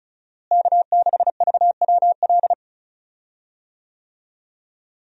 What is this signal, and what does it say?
Morse code, heard as a keyed tone, about 36 wpm, 700 Hz. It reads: K6VWL
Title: USA Call Signs - 5 Characters - Repeated 2x - 35wpm